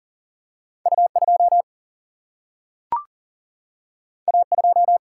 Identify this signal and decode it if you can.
Morse code, heard as a keyed tone, about 40 wpm, 700 Hz. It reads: U 2 A 2
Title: Single-Letter Single-Number 40wpm